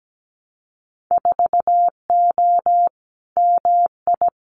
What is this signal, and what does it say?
Morse code, heard as a keyed tone, about 17 wpm, 700 Hz. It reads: 4O MI
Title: Winter Field Day Exchanges Encore 17wpm